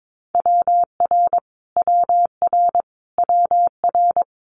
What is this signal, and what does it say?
Morse code, heard as a keyed tone, about 22 wpm, 700 Hz. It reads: WR WR WR
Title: Multi-letter Phonemes Repeated 3x 22wpm